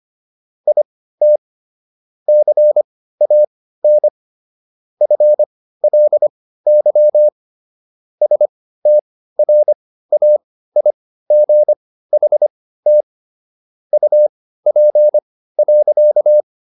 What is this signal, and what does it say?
Morse code, heard as a keyed tone, about 25 wpm, spaced as for 15 wpm, 600 Hz. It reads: IT CAN FLY STRAIGHT UP.